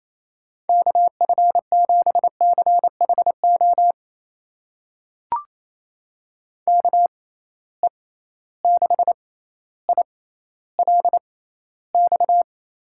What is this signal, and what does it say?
Morse code, heard as a keyed tone, about 28 wpm, spaced as for 14 wpm, 700 Hz. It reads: KF7CHO KE6ILX
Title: USA Call Signs - 6 Characters - 6x Character Spacing - 28wpm